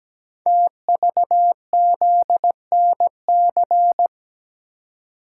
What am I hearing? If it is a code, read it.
Morse code, heard as a keyed tone, about 17 wpm, 700 Hz. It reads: TVZNC
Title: Code Groups - 5 Letter:Numbers - 17wpm